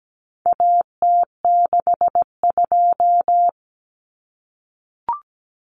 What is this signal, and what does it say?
Morse code, heard as a keyed tone, about 17 wpm, 700 Hz. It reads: AT62
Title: Common Cut Numbers - 4-Digit 17wpm